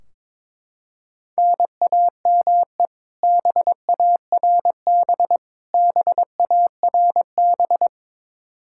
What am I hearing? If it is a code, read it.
Morse code, heard as a keyed tone, about 22 wpm, 700 Hz. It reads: NAME BARB BARB
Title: Practice QSO - Exchange 1 - Beginning - 22wpm